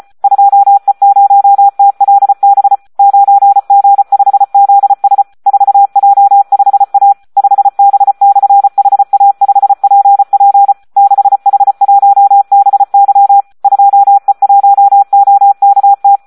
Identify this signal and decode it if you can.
Morse code, heard as a keyed tone, about 34 wpm, 800 Hz. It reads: 2E0TLB 9G5ZS 4J5A 5B/HA5PP 6H1BY 2E1OKT
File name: Audio practica CW / SD 1 34